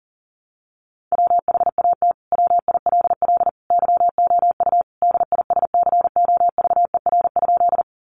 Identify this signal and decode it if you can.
Morse code, heard as a keyed tone, about 37 wpm, 700 Hz. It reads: WHAT WILL YOU DISCOVER?
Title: Sentences from Top 1100 Words 40wpm